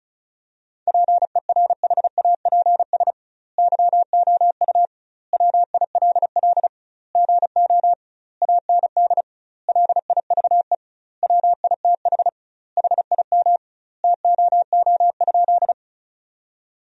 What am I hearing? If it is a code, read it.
Morse code, heard as a keyed tone, about 36 wpm, 700 Hz. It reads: PERHAPS YOU WILL GO AND LIVE WITH HIM TOO?
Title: 2x Word-Spacing - Sentences from Top 500 Words - 35wpm